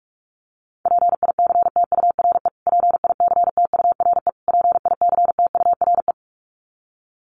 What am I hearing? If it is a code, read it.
Morse code, heard as a keyed tone, about 44 wpm, 700 Hz. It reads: PICTURE PICTURE PICTURE
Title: Top 200 Words Repeated 45wpm